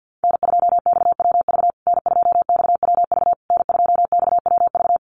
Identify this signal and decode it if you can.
Morse code, heard as a keyed tone, about 50 wpm, 700 Hz. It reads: N2XWV N2XWV N2XWV
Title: USA Call Signs Repeated 3x 50wpm